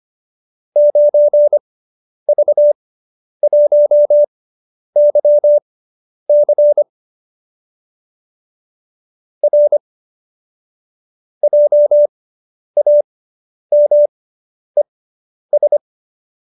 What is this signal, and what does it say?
Morse code, heard as a keyed tone, about 25 wpm, spaced as for 10 wpm, 590 Hz. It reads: 9V1YC R JAMES